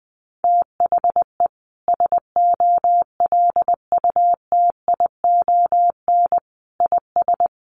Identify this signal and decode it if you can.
Morse code, heard as a keyed tone, about 20 wpm, 700 Hz. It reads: THE SOLUTION IS
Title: Sentences from Top 1300 Words 20wpm